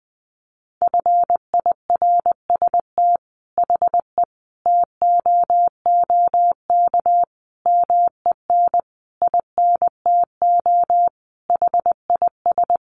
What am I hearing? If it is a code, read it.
Morse code, heard as a keyed tone, about 20 wpm, 700 Hz. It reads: FIRST HE TOOK MEN INTO HIS
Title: Sentences from Top 1500 Words 20wpm